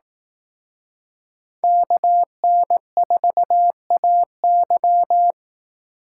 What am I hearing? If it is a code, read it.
Morse code, heard as a keyed tone, about 18 wpm, 700 Hz. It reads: KN4AY
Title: Speed Racing - USA 2x2 Call Signs 15wpm